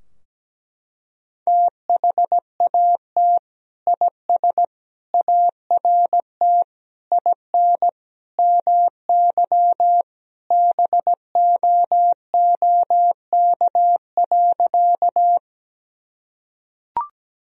Morse code, heard as a keyed tone, about 17 wpm, 700 Hz. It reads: THAT IS ART IN MY BOOK.
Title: Sentences from Top 1000 Words - Review Part 3 - 17wpm